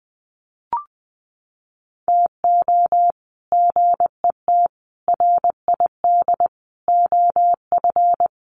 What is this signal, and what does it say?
Morse code, heard as a keyed tone, about 20 wpm, 700 Hz. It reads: TO GET RID OF
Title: Sets of 5 Words 20wpm